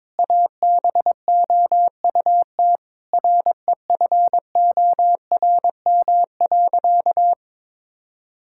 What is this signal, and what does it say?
Morse code, heard as a keyed tone, about 22 wpm, 700 Hz. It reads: ABOUT REFORM.
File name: Sentences with 7-Word Phrases - 22wpm